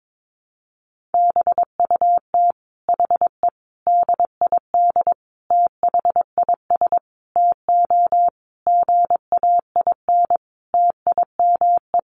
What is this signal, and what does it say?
Morse code, heard as a keyed tone, about 22 wpm, 700 Hz. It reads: BUT HE DID THIS TO GAIN TIME
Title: Sentences from Top 1400 Words 22wpm